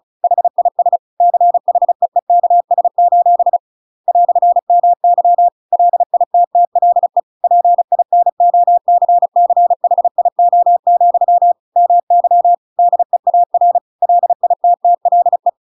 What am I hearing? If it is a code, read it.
Morse code, heard as a keyed tone, about 35 wpm, 700 Hz. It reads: HIS CHEEKS: "MY LITTLE PINOCCHIO, MY DEAR LITTLE
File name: Pinocchio / Pinocchio ch7 35 35 ik1hge